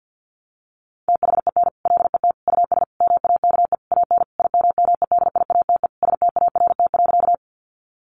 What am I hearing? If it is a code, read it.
Morse code, heard as a keyed tone, about 50 wpm, 700 Hz. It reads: THEN LET US MAKE AN IMMEDIATE START.
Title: Sentences from Top 1200 Words 50wpm